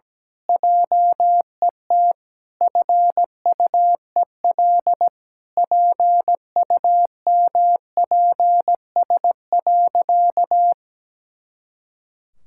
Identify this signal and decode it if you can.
Morse code, heard as a keyed tone, about 17 wpm, 700 Hz. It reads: JET FUEL PUMPS.
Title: Pangram Challenge v2 - 17wpm